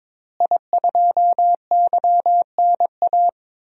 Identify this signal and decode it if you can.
Morse code, heard as a keyed tone, about 21 wpm, 700 Hz. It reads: I2YNA
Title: DXCC-100 Call Signs - 5 Characters - Repeated 2x - 22wpm